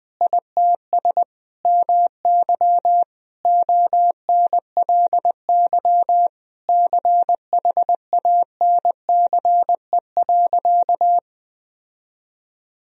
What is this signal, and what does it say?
Morse code, heard as a keyed tone, about 20 wpm, 700 Hz. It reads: ITS MY ONLY CHANCE.
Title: Sentences from Top 1000 Words - Review Part 4 - 20wpm